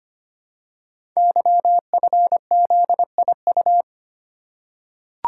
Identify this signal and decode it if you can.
Morse code, heard as a keyed tone, about 25 wpm, 700 Hz. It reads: YFZIU
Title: Code Groups - 5 Letters - 25wpm